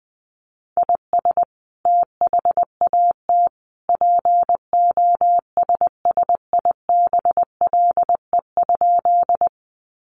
Morse code, heard as a keyed tone, about 20 wpm, 700 Hz. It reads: IS THAT POSSIBLE?
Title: Sentences from Top 500 Words 20wpm